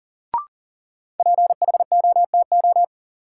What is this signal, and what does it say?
Morse code, heard as a keyed tone, about 37 wpm, 700 Hz. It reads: PHOTO
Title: Sentences from Top 1200 Words - New Word Drill - 40wpm